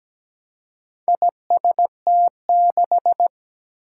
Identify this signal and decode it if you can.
Morse code, heard as a keyed tone, about 17 wpm, 700 Hz. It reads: IST6
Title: Four Challenging Letter-Number Combinations 17wpm